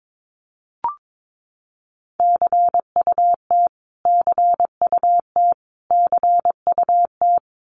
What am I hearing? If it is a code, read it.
Morse code, heard as a keyed tone, about 22 wpm, 700 Hz. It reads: CUT CUT CUT
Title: Top 500 Words Repeated 22wpm